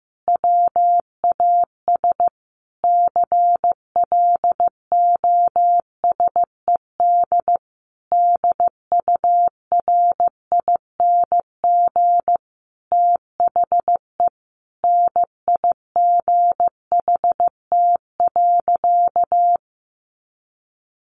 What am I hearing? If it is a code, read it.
Morse code, heard as a keyed tone, about 15 wpm, 700 Hz. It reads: WAS CLOSED DURING THE NIGHT.
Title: Sentences from Top 1400 Words 15wpm